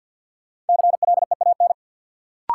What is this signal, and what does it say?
Morse code, heard as a keyed tone, about 50 wpm, 700 Hz. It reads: CLEAN
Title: Sentences from Top 1100 Words - New Word Drill - 50wpm